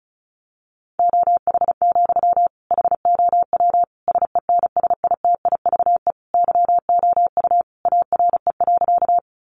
Morse code, heard as a keyed tone, about 34 wpm, 700 Hz. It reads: OH, HOW SENSITIVE YOU ARE.
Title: Sentences from Top 1400 Words 35wpm